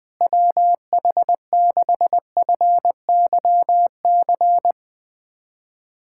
Morse code, heard as a keyed tone, about 20 wpm, 700 Hz. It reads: WH6FYC
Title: USA Callsign DE Callsign 20wpm